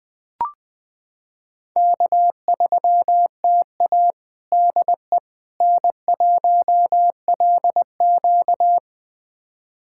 Morse code, heard as a keyed tone, about 20 wpm, 700 Hz. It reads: K3TA DE N1LQ
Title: USA Callsign DE Callsign - 4 Characters - 20wpm